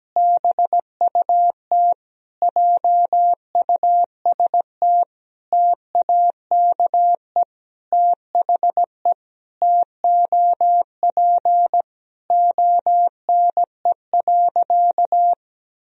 Morse code, heard as a keyed tone, about 17 wpm, 700 Hz. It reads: BUT JUST TAKE THE TOP ONE.
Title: Sentences from Top 1000 Words - Review Part 2 - 17wpm